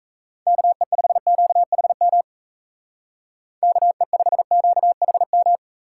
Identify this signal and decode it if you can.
Morse code, heard as a keyed tone, about 34 wpm, 700 Hz. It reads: KE5QHM KE5QHM
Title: Speed Racing - USA 2x3 Call Signs 28wpm